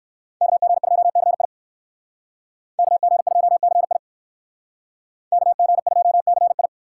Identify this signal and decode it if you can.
Morse code, heard as a keyed tone, about 56 wpm, 700 Hz. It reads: KG2CI KG2CI KG2CI
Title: Speed Racing - USA Call Signs 45wpm